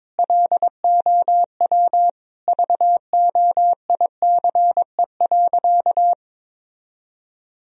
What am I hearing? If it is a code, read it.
Morse code, heard as a keyed tone, about 22 wpm, 700 Hz. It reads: LOW VOICE.
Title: Sentences from Top 1000 Words - Review Part 1 - 22wpm